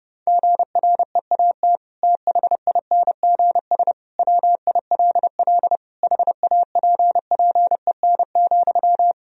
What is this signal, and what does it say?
Morse code, heard as a keyed tone, about 30 wpm, 700 Hz. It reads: GREAT THINGS WILL HAPPEN,
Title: Sentences from Top 1000 Words - Review Part 4 - 30wpm